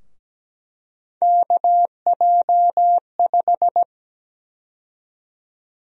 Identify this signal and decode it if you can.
Morse code, heard as a keyed tone, about 17 wpm, 700 Hz. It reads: KJ5
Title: USA Call Sign Prefixes - Repeated 2x - 17wpm